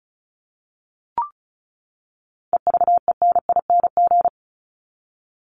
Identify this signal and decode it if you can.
Morse code, heard as a keyed tone, about 34 wpm, 700 Hz. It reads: EVENING
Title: Top 1501-1750 Words 35wpm